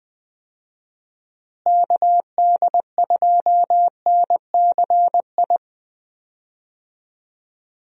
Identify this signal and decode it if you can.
Morse code, heard as a keyed tone, about 20 wpm, 700 Hz. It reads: KD2NCI
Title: USA Call Signs - 6 Characters - Repeated 2x - 20wpm